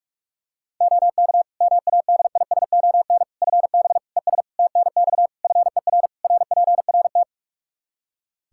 Morse code, heard as a keyed tone, about 41 wpm, 700 Hz. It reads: OK MADISON FB ES TNX FER RPRT
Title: Practice QSO - Exchange 2 - Info - 45wpm